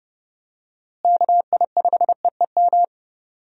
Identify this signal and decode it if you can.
Morse code, heard as a keyed tone, about 30 wpm, 700 Hz. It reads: KI5EEM
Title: USA Call Signs - 6 Characters - Repeated 2x - 30wpm